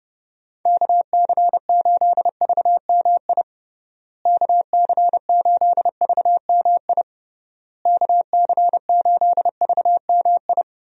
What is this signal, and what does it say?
Morse code, heard as a keyed tone, about 32 wpm, 700 Hz. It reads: KC8VMI KC8VMI KC8VMI
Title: USA Call Signs - 6 Characters - Repeated 3x - 30wpm